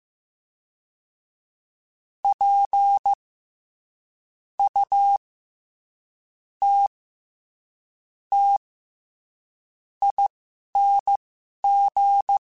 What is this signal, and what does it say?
Morse code, heard as a keyed tone, about 15 wpm, spaced as for 11 wpm, 800 Hz. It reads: P U T T ING
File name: Special INGs-mp3s / Special INGs-mp3s / SlowToFaster ing @10